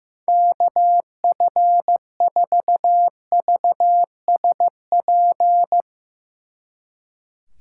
Morse code, heard as a keyed tone, about 16 wpm, 700 Hz. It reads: KF4VSP
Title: Speed Racing - USA 2x3 Call Signs 15wpm